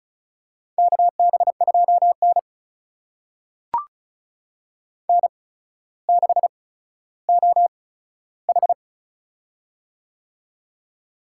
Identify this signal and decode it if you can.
Morse code, heard as a keyed tone, about 36 wpm, spaced as for 15 wpm, 700 Hz. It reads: KB2N N6OH
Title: USA Call Signs - 4 Characters - 8x Character Spacing - 35wpm